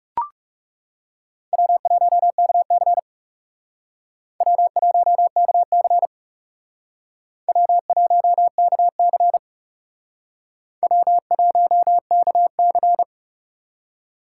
Morse code, heard as a keyed tone, about 36 wpm, 700 Hz. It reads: W1KC W1KC W1KC W1KC
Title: Speed Racing - USA 1x2 Call Signs 30wpm